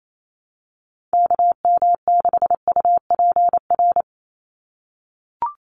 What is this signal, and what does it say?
Morse code, heard as a keyed tone, about 28 wpm, 700 Hz. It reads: KM6UPR
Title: USA Call Signs 28wpm